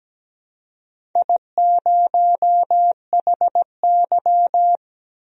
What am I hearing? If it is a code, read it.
Morse code, heard as a keyed tone, about 17 wpm, 700 Hz. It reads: I0HY
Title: DXCC-100 Call Signs - 4 Characters - 17wpm